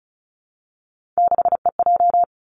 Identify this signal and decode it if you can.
Morse code, heard as a keyed tone, about 36 wpm, 700 Hz. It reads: 6EJ
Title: Three Challenging Letter-Number Combinations 35wpm